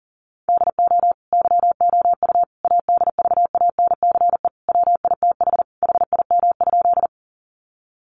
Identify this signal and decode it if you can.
Morse code, heard as a keyed tone, about 40 wpm, 700 Hz. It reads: DO YOU ADVANCE WITH HIM?
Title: Sentences from Top 1400 Words 40wpm